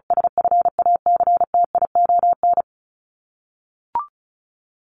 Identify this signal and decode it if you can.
Morse code, heard as a keyed tone, about 34 wpm, 700 Hz. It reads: SFACTION
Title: Top 1751-2000 Words 35wpm